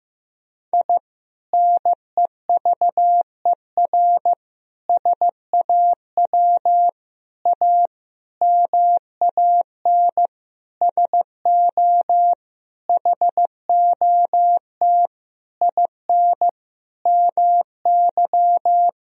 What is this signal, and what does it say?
Morse code, heard as a keyed tone, about 15 wpm, 700 Hz. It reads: I NEVER SAW A MAN SO HOT IN MY
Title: Sentences from Top 1000 Words - Review Part 3 - 15wpm